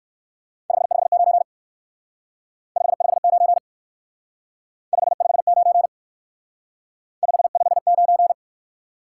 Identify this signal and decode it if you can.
Morse code, heard as a keyed tone, about 56 wpm, 700 Hz. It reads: 559 559 559 559